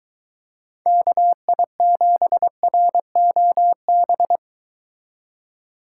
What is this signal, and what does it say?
Morse code, heard as a keyed tone, about 23 wpm, 700 Hz. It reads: KI7ROB